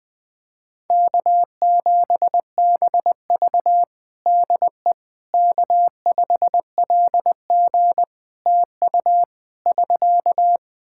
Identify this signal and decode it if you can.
Morse code, heard as a keyed tone, about 20 wpm, 700 Hz. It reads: K7BV DE K5LG TU <SK>
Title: K1USN SST QSO Practice 20wpm